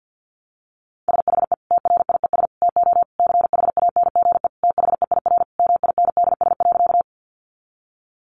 Morse code, heard as a keyed tone, about 48 wpm, 700 Hz. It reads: SHE TRIES TO CHANGE THEIR MINDS.
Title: Easy Phrases 50wpm